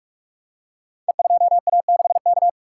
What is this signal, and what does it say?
Morse code, heard as a keyed tone, about 44 wpm, 700 Hz. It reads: E2A6K
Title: Code Groups - 5 Letter:Numbers - 45wpm